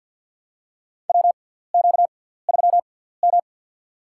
Morse code, heard as a keyed tone, about 52 wpm, spaced as for 17 wpm, 700 Hz. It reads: WQ3M
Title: USA Call Signs - 4 Characters - 6x Character Spacing - 50wpm